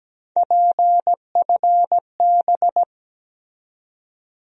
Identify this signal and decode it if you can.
Morse code, heard as a keyed tone, about 17 wpm, 700 Hz. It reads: PFB
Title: USA Call Sign Postfixes - 3 Letters 17wpm